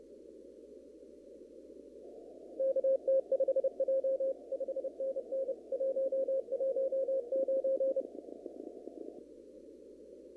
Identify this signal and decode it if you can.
Morse code, heard as a keyed tone, about 30 wpm, 530 Hz. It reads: KT5J 5NN 119
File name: CW / kt5j wpx cw-op-k5pi